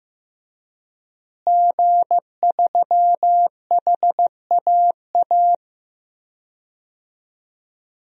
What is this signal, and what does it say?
Morse code, heard as a keyed tone, about 15 wpm, 700 Hz. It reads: G3HAA
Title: DXCC-100 Call Signs - 5 Characters - Repeated 2x - 15wpm